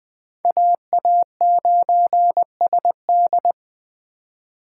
Morse code, heard as a keyed tone, about 21 wpm, 700 Hz. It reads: AA9SD